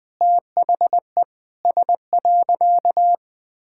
Morse code, heard as a keyed tone, about 20 wpm, 700 Hz. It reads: THE S.
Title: Sentences with 6-Word Phrases - 20wpm